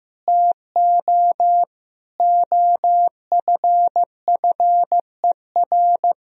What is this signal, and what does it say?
Morse code, heard as a keyed tone, about 15 wpm, 700 Hz. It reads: TO OFFER
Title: Sentences from Top 1000 Words - Review Part 4 - 15wpm